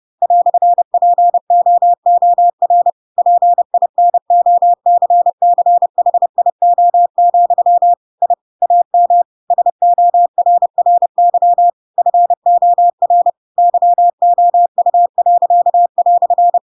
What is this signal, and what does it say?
Morse code, heard as a keyed tone, about 30 wpm, 700 Hz. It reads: "POOR PINOCCHIO, I AM SORRY FOR YOU."
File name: Pinocchio / Pinocchio ch4 30 30 ik1hge